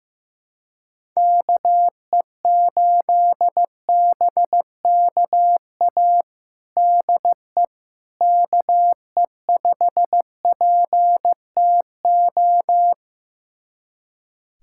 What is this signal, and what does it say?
Morse code, heard as a keyed tone, about 15 wpm, 700 Hz. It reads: KE8BKA DE KE5PTO
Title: USA Callsign DE Callsign - 6 Characters - 15wpm